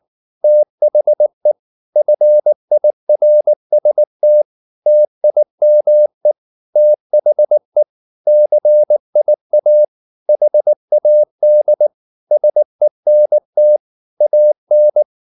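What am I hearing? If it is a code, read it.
Morse code, heard as a keyed tone, about 19 wpm, 600 Hz. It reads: THE FIRST TIME THE CIA HAD SENT AN
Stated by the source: Morse code at 19 wpm from CW med Gnister, reading Ken Follett's novel Never in English.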